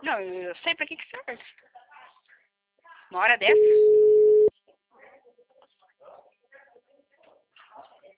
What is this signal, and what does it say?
Ringback tone (Europe, 425 Hz), with speech.